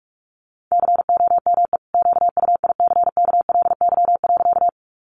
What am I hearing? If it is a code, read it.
Morse code, heard as a keyed tone, about 44 wpm, 700 Hz. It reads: COME QUICKLY.
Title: Sentences from Top 700 Words 45wpm